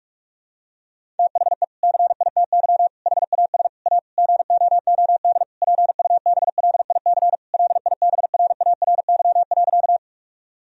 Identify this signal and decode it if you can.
Morse code, heard as a keyed tone, about 43 wpm, 700 Hz. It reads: THE CITY HAS A GOOD PUBLIC LIBRARY.
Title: Sentences from Top 1000 Words - Review Part 4 - 45wpm